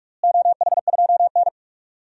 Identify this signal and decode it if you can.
Morse code, heard as a keyed tone, about 46 wpm, 700 Hz. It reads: OH2N
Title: DXCC-100 Call Signs - 4 Characters - Repeated 2x - 45wpm